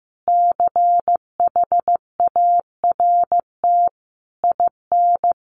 Morse code, heard as a keyed tone, about 15 wpm, 700 Hz. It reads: CHART IN
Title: Sentences from Top 1300 Words 15wpm